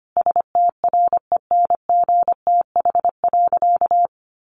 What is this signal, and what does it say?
Morse code, heard as a keyed tone, about 25 wpm, 700 Hz. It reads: STRENGTH.
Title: Sentences from Top 1300 Words 25wpm